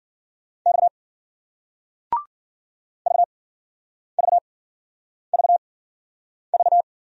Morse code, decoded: X V V V V

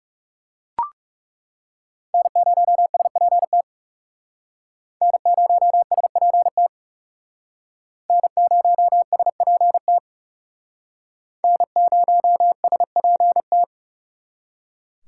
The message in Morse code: N0SPT N0SPT N0SPT N0SPT

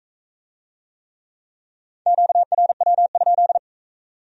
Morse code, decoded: QRW?

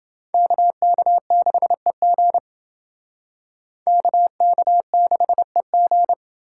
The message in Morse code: KK6EG KK6EG

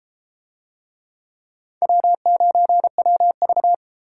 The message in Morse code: W9WV

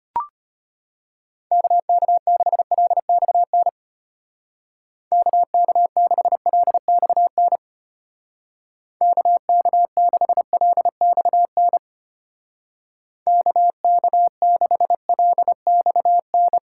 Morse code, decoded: KK6LXN KK6LXN KK6LXN KK6LXN

